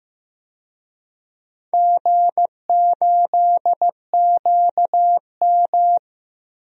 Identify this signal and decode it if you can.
Morse code, heard as a keyed tone, about 15 wpm, 700 Hz. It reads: G8QM
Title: DXCC-100 Call Signs - 4 Characters - 15wpm